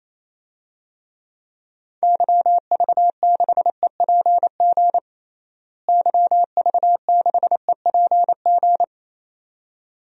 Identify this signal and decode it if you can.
Morse code, heard as a keyed tone, about 27 wpm, 700 Hz. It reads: YV6EPG YV6EPG
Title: DXCC-100 Call Signs - 6 Characters - Repeated 2x - 28wpm